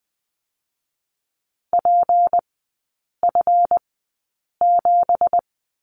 Morse code, heard as a keyed tone, about 20 wpm, spaced as for 8 wpm, 700 Hz. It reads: PF7
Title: 166 Lesson 27e - Number 7 - Three Characters 20wpm